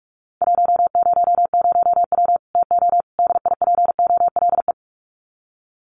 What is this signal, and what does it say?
Morse code, heard as a keyed tone, about 44 wpm, 700 Hz. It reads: 100W TO DIPOLE
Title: Common QSO Phrases 45wpm